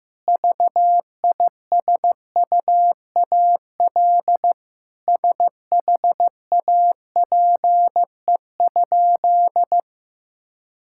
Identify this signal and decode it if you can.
Morse code, heard as a keyed tone, about 15 wpm, 700 Hz. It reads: VISUAL SHAPE?
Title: Sentences from Top 1000 Words - Review Part 6 - 15wpm